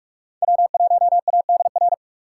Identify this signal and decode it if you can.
Morse code, heard as a keyed tone, about 39 wpm, 700 Hz. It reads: W1ADR